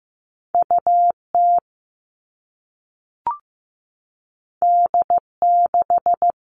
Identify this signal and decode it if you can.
Morse code, heard as a keyed tone, about 15 wpm, 700 Hz. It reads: UT D6
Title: All Cut Numbers - 2-Digit 15wpm